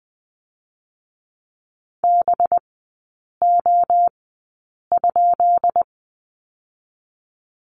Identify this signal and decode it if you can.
Morse code, heard as a keyed tone, about 20 wpm, spaced as for 8 wpm, 700 Hz. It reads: BO?
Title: 184 Lesson 30e - Letter b - Three Characters 20wpm